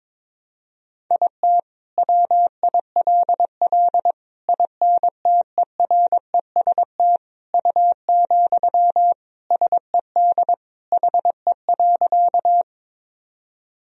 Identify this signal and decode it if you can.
Morse code, heard as a keyed tone, about 22 wpm, 700 Hz. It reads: IT WILL INTEREST U, SED HE.
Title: Sentences from Top 400 Words and Abbreviations 22wpm